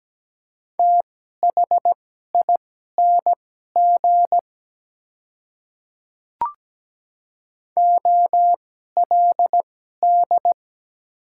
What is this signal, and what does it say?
Morse code, heard as a keyed tone, about 17 wpm, spaced as for 12 wpm, 700 Hz. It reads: THING OLD
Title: 2x Character Spacing - Top 300 Words - 17wpm